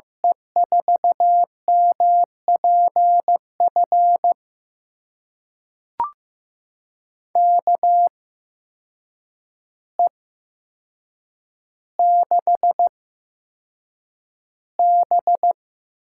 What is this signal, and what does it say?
Morse code, heard as a keyed tone, about 15 wpm, spaced as for 7 wpm, 700 Hz. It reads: E4MPF KE6B
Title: USA Call Signs - 6 Characters - 8x Character Spacing - 15wpm